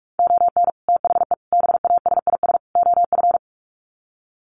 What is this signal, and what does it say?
Morse code, heard as a keyed tone, about 44 wpm, 700 Hz. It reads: ON THE BASIS OF
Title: Sets of 4 Words 45wpm